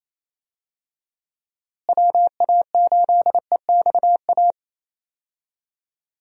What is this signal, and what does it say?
Morse code, heard as a keyed tone, about 28 wpm, 700 Hz. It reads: WA8EXA
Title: USA Call Signs - 6 Characters - Repeated 2x - 28wpm